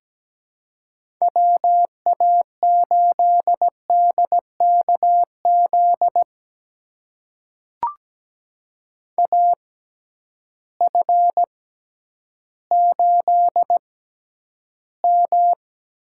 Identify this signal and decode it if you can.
Morse code, heard as a keyed tone, about 17 wpm, spaced as for 10 wpm, 700 Hz. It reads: WA8DKZ AF8M
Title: USA Call Signs - 6x Character Spacing - 17wpm